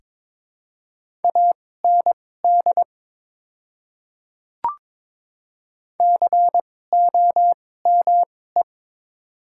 Morse code, heard as a keyed tone, about 22 wpm, spaced as for 16 wpm, 700 Hz. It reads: AND COME
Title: 2x Character Spacing - Top 500 Words - 22wpm